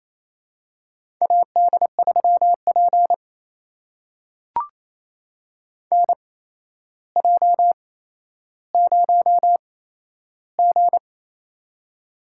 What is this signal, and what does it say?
Morse code, heard as a keyed tone, about 28 wpm, spaced as for 12 wpm, 700 Hz. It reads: AD3P NJ0G
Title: USA Call Signs - 4 Characters - 8x Character Spacing - 28wpm